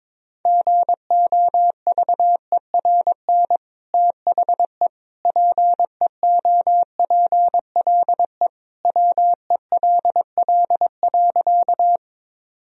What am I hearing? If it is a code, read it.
Morse code, heard as a keyed tone, about 22 wpm, 700 Hz. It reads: GOVERN THE PEOPLE WELL.
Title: Sentences from Top 1000 Words - Review Part 2 - 22wpm